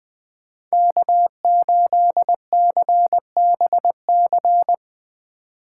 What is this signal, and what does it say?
Morse code, heard as a keyed tone, about 20 wpm, 700 Hz. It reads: K8CBC